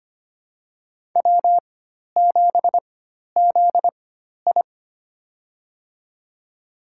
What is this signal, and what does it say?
Morse code, heard as a keyed tone, about 25 wpm, spaced as for 12 wpm, 700 Hz. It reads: W7ZI
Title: USA Call Signs - 4 Characters - 4x Character Spacing - 25wpm